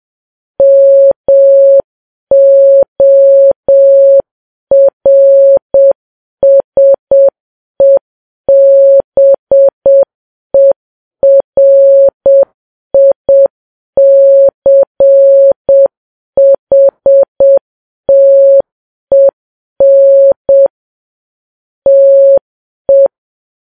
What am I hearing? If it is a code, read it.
Morse code, heard as a keyed tone, about 7 wpm, 550 Hz. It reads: MORSEBERICHTEN TE
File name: Files Web tls / Morse on9fv / morse7kps